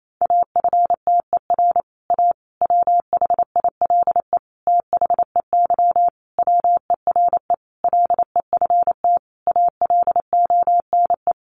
Morse code, heard as a keyed tone, about 28 wpm, 700 Hz. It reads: AFTER A WHILE THEY WERE LEFT ALONE
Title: Sentences from Top 1000 Words 28wpm